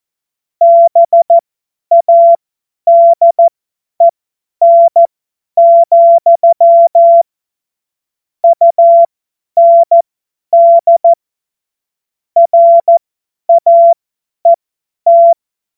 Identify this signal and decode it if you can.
Morse code, heard as a keyed tone, about 14 wpm, spaced as for 10 wpm, 680 Hz. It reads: BADEN, UND RAET
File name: Morsekurs / Witze / Witz 15. Februar 2022 50 F70